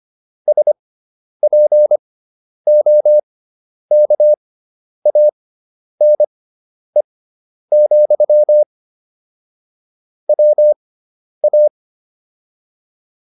Morse code, heard as a keyed tone, about 25 wpm, spaced as for 10 wpm, 600 Hz. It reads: SPOKANE, WA
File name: QSO310 10